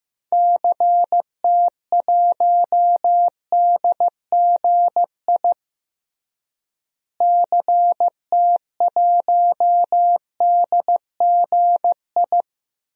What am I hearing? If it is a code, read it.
Morse code, heard as a keyed tone, about 15 wpm, 700 Hz. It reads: CT1DGI CT1DGI
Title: DXCC-100 Call Signs - 6 Characters - Repeated 2x - 15wpm